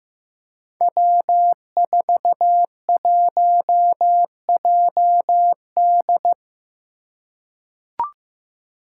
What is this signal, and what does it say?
Morse code, heard as a keyed tone, about 15 wpm, 700 Hz. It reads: W41JD
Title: Five Challenging Letter-Number Combinations 15wpm